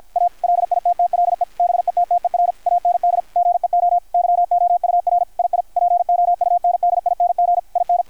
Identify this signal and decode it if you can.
Morse code, heard as a keyed tone, about 53 wpm, 700 Hz, with radio noise. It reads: A LITTLE BETTER AND GEO YOUR II POUNDING IN
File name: archived / CRSnet